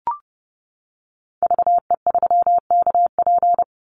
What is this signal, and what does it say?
Morse code, heard as a keyed tone, about 30 wpm, 700 Hz. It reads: VE3KP
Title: Call Signs - 30wpm